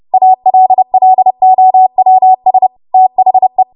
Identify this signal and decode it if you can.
Morse code, heard as a keyed tone, about 30 wpm, 750 Hz. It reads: ALLOWS THE
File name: Morse / 30WPM